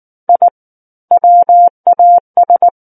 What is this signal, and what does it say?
Morse code, heard as a keyed tone, about 19 wpm, 700 Hz. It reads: I WAS
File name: Book One / 10-Chapter 10 - 19 WPM